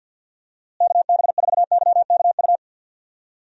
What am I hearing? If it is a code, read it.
Morse code, heard as a keyed tone, about 48 wpm, 700 Hz. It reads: KB4YKU
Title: USA Call Signs - 6 Characters - Repeated 2x - 50wpm